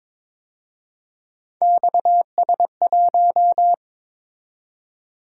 Morse code, heard as a keyed tone, about 26 wpm, 700 Hz. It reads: XS1
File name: Speed-Racing - Three Letter-Number Combinations - 22wpm